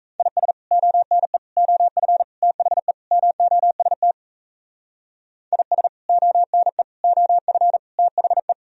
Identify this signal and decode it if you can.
Morse code, heard as a keyed tone, about 36 wpm, 700 Hz. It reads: IS ONE OF THE MOST IS ONE OF THE